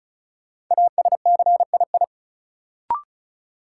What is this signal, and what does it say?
Morse code, heard as a keyed tone, about 36 wpm, 700 Hz. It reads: ASCII
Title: Sets of 5 Words - New Word Drill - 35wpm